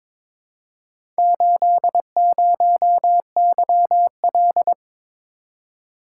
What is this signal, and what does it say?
Morse code, heard as a keyed tone, about 22 wpm, 700 Hz. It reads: 80YL
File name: practice / 22wpm / Four Letter-Number Combinations - 22wpm